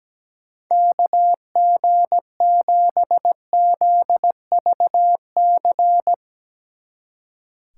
Morse code, heard as a keyed tone, about 17 wpm, 700 Hz. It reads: KG7ZVC
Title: Speed Racing - USA 2x3 Call Signs 17wpm